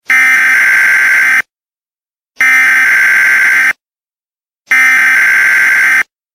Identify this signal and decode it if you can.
Emergency Alert System (SAME) tones (header bursts), heard as harsh buzzing bursts. It is heard from 0.1 to 6.0 s.
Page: eas-alarm-weather-alert